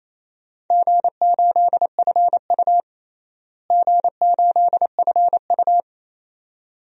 Morse code, heard as a keyed tone, about 28 wpm, 700 Hz. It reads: G8FU G8FU
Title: DXCC-100 Call Signs - 4 Characters - Repeated 2x - 28wpm